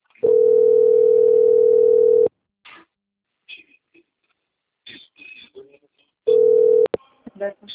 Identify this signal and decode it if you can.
Ringback tone (North America), 440 + 480 Hz.
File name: out- - - - - .4457